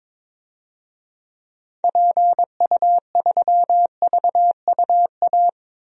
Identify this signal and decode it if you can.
Morse code, heard as a keyed tone, about 22 wpm, 700 Hz. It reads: PU3VUA
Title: DXCC-100 Call Signs - 6 Characters - 22wpm